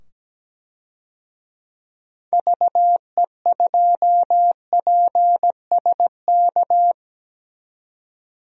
Morse code, heard as a keyed tone, about 17 wpm, 700 Hz. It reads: VE2PSK
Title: DXCC-100 Call Signs - 6 Characters - 17wpm